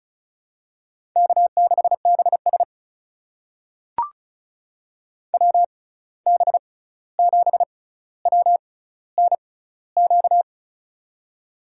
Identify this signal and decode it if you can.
Morse code, heard as a keyed tone, about 34 wpm, spaced as for 16 wpm, 700 Hz. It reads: K6BS WB7WNQ
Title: USA Call Signs - 6x Character Spacing - 35wpm